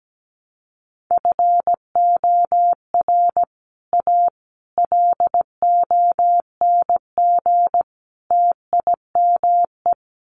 Morse code, heard as a keyed tone, about 17 wpm, 700 Hz. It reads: FOR A LONG TIME
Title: Sets of 4 Words 17wpm